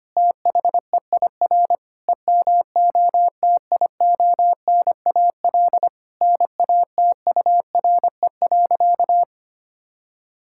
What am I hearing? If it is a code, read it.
Morse code, heard as a keyed tone, about 25 wpm, 700 Hz. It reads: THEIR EMOTIONAL NATURE.
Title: Sentences from Top 1000 Words - Review Part 5 - 25wpm